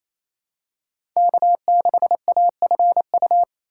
Morse code, heard as a keyed tone, about 28 wpm, 700 Hz. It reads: K6AFU
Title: USA Call Signs - 5 Characters - 4x Character Spacing - 28wpm